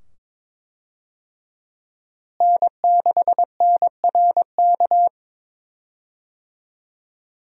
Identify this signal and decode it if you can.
Morse code, heard as a keyed tone, about 22 wpm, 700 Hz. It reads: N6NRK
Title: USA Call Signs - 5 Characters - Repeated 2x - 22wpm